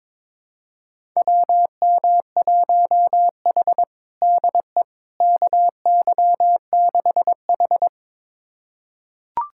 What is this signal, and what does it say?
Morse code, heard as a keyed tone, about 21 wpm, 700 Hz. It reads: WM1H DE KY6H
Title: USA Callsign DE Callsign - 4 Characters - 22wpm